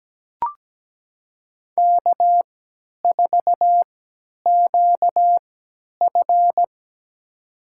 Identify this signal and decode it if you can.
Morse code, heard as a keyed tone, about 17 wpm, spaced as for 10 wpm, 700 Hz. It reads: K4QF
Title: USA Call Signs - 3x Character Spacing - 17wpm